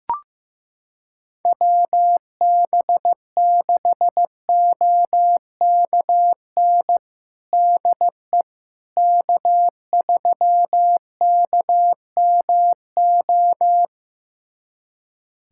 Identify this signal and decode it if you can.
Morse code, heard as a keyed tone, about 15 wpm, 700 Hz. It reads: WB6OKN DE K3KMO
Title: USA Callsign DE Callsign 15wpm